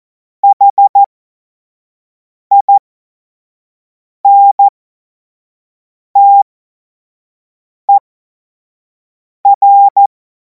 Morse code, heard as a keyed tone, about 14 wpm, spaced as for 5 wpm, 800 Hz. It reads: HINTER
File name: Morsekurs / Witze / Witz 07. Juli 2026 25 F70